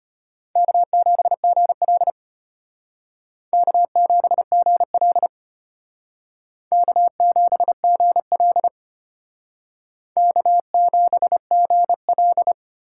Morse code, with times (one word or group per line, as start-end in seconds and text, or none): 0.6-2.1 K7GL
3.5-5.3 K7GL
6.7-8.7 K7GL
10.2-12.5 K7GL